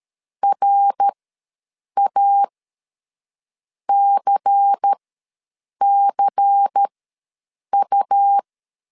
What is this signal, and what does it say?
Morse code, heard as a keyed tone, about 13 wpm, spaced as for 6 wpm, 800 Hz. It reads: RACCU
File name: LEZIONI / 5 LBVC / 5 LBVC ws 12wpm 3159